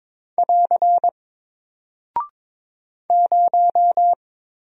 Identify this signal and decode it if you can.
Morse code, heard as a keyed tone, about 22 wpm, 700 Hz. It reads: <AR> 0
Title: Single Character (Letter:Number:Punctuation:Prosign) Alt 22wpm